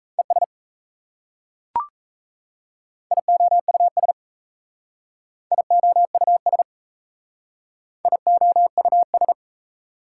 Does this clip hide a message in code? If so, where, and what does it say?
Yes: Morse code from 0.2 to 9.3 s, heard as a keyed tone, about 37 wpm, 690 Hz. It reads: ES IOUS IOUS IOUS